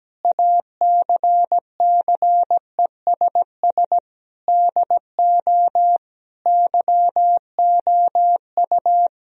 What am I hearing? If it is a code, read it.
Morse code, heard as a keyed tone, about 17 wpm, 700 Hz. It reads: ACCESS DO YOU
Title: Sentences from Top 1000 Words - Review Part 4 - 17wpm